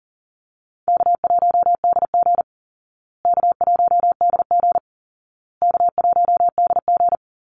Morse code, heard as a keyed tone, about 40 wpm, 700 Hz. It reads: K1DG K1DG K1DG
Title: Easy Call Signs Repeated 3x - 40wpm